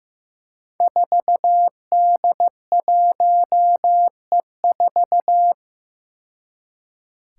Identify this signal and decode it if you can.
Morse code, heard as a keyed tone, about 15 wpm, 700 Hz. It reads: 4D1E4
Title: Five Challenging Letter-Number Combinations 15wpm